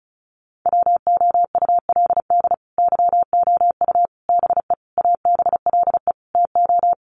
Morse code, heard as a keyed tone, about 34 wpm, 700 Hz. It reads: WOULD YOU BE ABLE TO
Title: Sentences from Top 1400 Words 35wpm